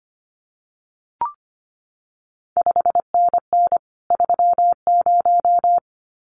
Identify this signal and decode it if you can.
Morse code, heard as a keyed tone, about 25 wpm, 700 Hz. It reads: 5NN 30
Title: CQ WW DX Contest Exchange Encore - 25wpm